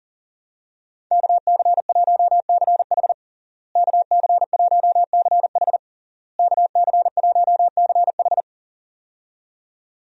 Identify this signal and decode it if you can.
Morse code, heard as a keyed tone, about 40 wpm, 700 Hz. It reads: KC1CH KC1CH KC1CH
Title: USA Call Signs - 5 Characters - Repeated 3x - 40wpm